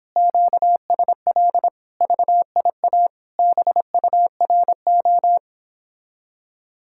Morse code, heard as a keyed tone, about 31 wpm, 700 Hz. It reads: QSL VIA BURO